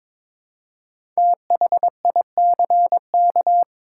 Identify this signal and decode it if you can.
Morse code, heard as a keyed tone, about 22 wpm, 700 Hz. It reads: THICK
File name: 5 Letter Words - First 3 Letters Given - 22wpm